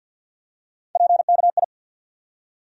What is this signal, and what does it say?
Morse code, heard as a keyed tone, about 52 wpm, 700 Hz. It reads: PKI
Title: USA Call Sign Postfixes - 3 Letters 50wpm